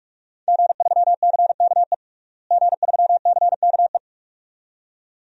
Morse code, heard as a keyed tone, about 42 wpm, 700 Hz. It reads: G3CKE G3CKE
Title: DXCC-100 Call Signs - 5 Characters - Repeated 2x - 45wpm